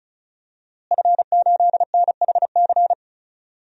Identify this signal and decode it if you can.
Morse code, heard as a keyed tone, about 36 wpm, 700 Hz. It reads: F8NHC
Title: DXCC-100 Call Signs - 5 Characters - Repeated 3x - 35wpm